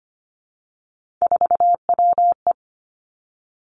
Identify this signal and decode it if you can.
Morse code, heard as a keyed tone, about 25 wpm, 700 Hz. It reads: 4WE
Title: Three Challenging Letter-Number Combinations 25wpm